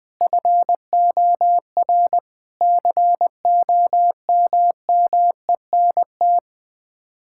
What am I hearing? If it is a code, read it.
Morse code, heard as a keyed tone, about 20 wpm, 700 Hz. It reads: FOR COMMENT
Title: Sets of 8 Words - 20wpm